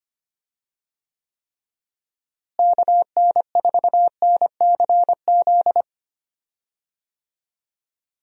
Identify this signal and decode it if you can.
Morse code, heard as a keyed tone, about 25 wpm, 700 Hz. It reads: KN4NCZ
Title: USA Call Signs - 6 Characters - Repeated 3x - 25wpm